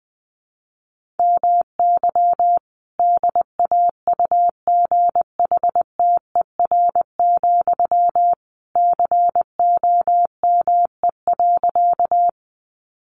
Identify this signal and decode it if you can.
Morse code, heard as a keyed tone, about 20 wpm, 700 Hz. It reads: MY DAUGHTER, COME.
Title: Sentences from Top 1500 Words 20wpm